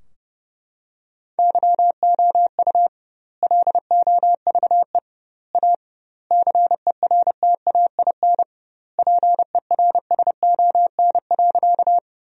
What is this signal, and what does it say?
Morse code, heard as a keyed tone, about 32 wpm, 700 Hz. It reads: YOU LOVE A CERTAIN PERSON.
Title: 2x Word-Spacing - Sentences from Top 500 Words - 30wpm